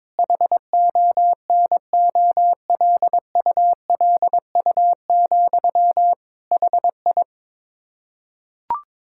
Morse code, heard as a keyed tone, about 22 wpm, 700 Hz. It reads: HONOLULU, HI
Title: K1USN SST QSO Practice 22wpm